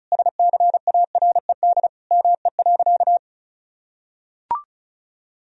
Morse code, heard as a keyed tone, about 34 wpm, 700 Hz. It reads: SCARED ME.
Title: Sentences from Top 1000 Words - Review Part 4 - 35wpm